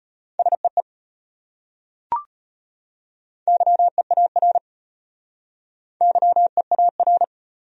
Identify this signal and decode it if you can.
Morse code, heard as a keyed tone, about 32 wpm, 700 Hz. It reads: SEE YEAR YEAR